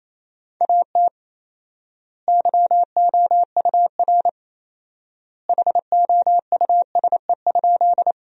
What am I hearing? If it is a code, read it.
Morse code, heard as a keyed tone, about 28 wpm, 700 Hz. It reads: AT YOUR HOUSE?
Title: 4x Word-Spacing - Sentences from Top 200 Words - 28wpm